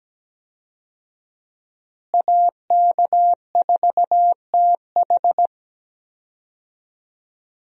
Morse code, heard as a keyed tone, about 17 wpm, 700 Hz. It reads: AK4TH
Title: USA Call Signs - 5 Characters - Repeated 3x - 17wpm